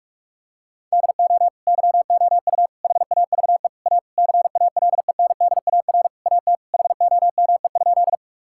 Morse code, heard as a keyed tone, about 41 wpm, 700 Hz. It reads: DO YOU HAVE A CALENDAR AT HOME?
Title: Sentences from Top 1600 Words 45wpm